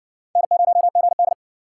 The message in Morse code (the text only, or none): N0GD